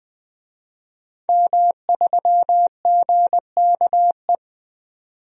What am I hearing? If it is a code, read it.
Morse code, heard as a keyed tone, about 20 wpm, 700 Hz. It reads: M3GKE
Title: DXCC-100 Call Signs - 5 Characters - 20wpm